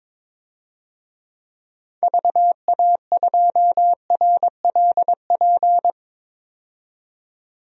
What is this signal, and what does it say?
Morse code, heard as a keyed tone, about 21 wpm, 700 Hz. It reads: VA2RLP
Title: DXCC-100 Call Signs - 6 Characters - 22wpm